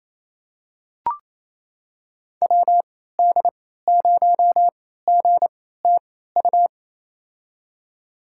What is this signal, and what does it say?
Morse code, heard as a keyed tone, about 28 wpm, spaced as for 16 wpm, 700 Hz. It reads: WD0GTU
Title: USA Call Signs - 3x Character Spacing - 28wpm